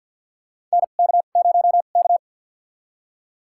Morse code, heard as a keyed tone, about 51 wpm, spaced as for 36 wpm, 700 Hz. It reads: NK0K
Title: USA Call Signs - 4 Characters - 2x Character Spacing - 50wpm